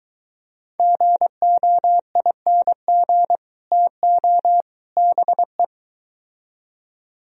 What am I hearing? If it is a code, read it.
Morse code, heard as a keyed tone, about 23 wpm, 700 Hz. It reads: GOING TO BE